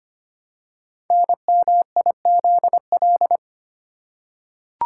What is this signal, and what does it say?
Morse code, heard as a keyed tone, about 25 wpm, 690 Hz. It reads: NMIZL
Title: Code Groups - 5 Letters - 25wpm